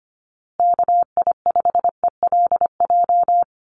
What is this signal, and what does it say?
Morse code, heard as a keyed tone, about 25 wpm, 700 Hz. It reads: KI5ELJ
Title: USA Call Signs Repeated 2x 25wpm